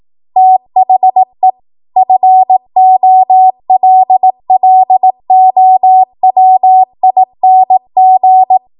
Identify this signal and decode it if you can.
Morse code, heard as a keyed tone, about 18 wpm, 750 Hz. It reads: THE FOLLOWING
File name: Archive / 18 WPM / 18WPM